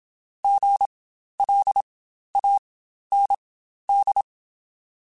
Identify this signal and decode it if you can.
Morse code, heard as a keyed tone, about 27 wpm, spaced as for 12 wpm, 800 Hz. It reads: GLAND
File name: Spaced Alphas / Spaced Alphas mp3s / Gs sp @25